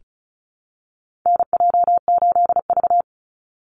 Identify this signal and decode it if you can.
Morse code, heard as a keyed tone, about 36 wpm, 700 Hz. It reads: NJ8V
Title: Call Signs - 35wpm